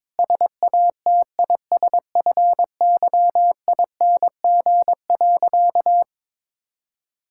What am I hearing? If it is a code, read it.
Morse code, heard as a keyed tone, about 22 wpm, 700 Hz. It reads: SATISFYING.
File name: practice / 22wpm / Sentences with 3-Word Phrases - 22wpm